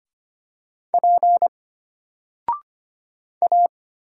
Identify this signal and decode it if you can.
Morse code, heard as a keyed tone, about 25 wpm, 700 Hz. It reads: P A
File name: Single-Letter-Number-25wpm